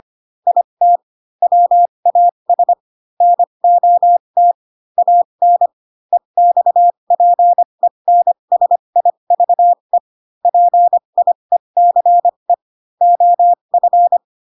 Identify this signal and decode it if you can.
Morse code, heard as a keyed tone, about 25 wpm, spaced as for 22 wpm, 700 Hz. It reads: IT WAS NOT AN EXPENSIVE PIECE OF